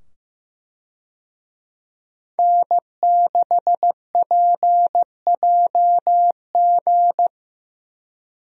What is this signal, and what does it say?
Morse code, heard as a keyed tone, about 15 wpm, 700 Hz. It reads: N6PJG
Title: USA Call Signs - 5 Characters - 15wpm